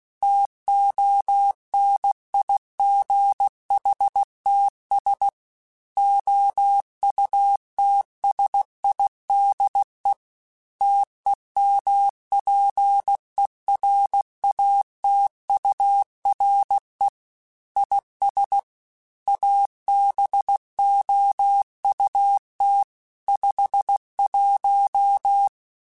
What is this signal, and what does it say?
Morse code, heard as a keyed tone, about 16 wpm, 800 Hz. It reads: TONIGHTS OUTSIDE TEMPERATURE IS ABOUT 51
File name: Weather Sets / Weather Sets-mp3s / Weather 4 21 @15